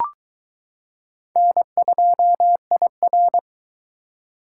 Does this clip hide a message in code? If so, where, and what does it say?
Yes: Morse code from 1.4 to 3.4 s, heard as a keyed tone, about 21 wpm, 700 Hz. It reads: N2IR